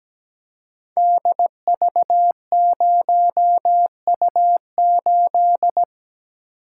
Morse code, heard as a keyed tone, about 17 wpm, 690 Hz. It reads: DV0U8
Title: Code Groups - 5 Letter:Numbers - 17wpm